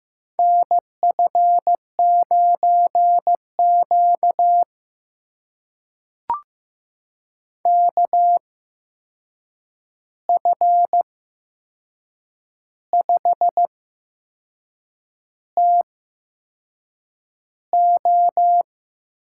Morse code, heard as a keyed tone, about 15 wpm, spaced as for 6 wpm, 700 Hz. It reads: NF9Q KF5TO
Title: USA Call Signs - 8x Character Spacing - 15wpm